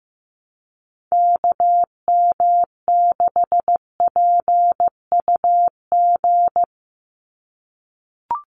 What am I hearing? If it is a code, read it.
Morse code, heard as a keyed tone, about 15 wpm, 700 Hz. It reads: KM6PUG
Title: USA Call Signs 15wpm